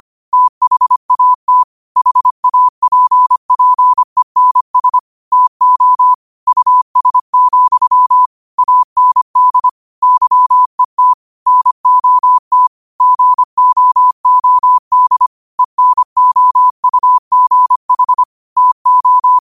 Morse code, heard as a keyed tone, about 25 wpm, 1000 Hz. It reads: THAT HAPPENS TO US, AND YET NOT GOOD ENOUGH TO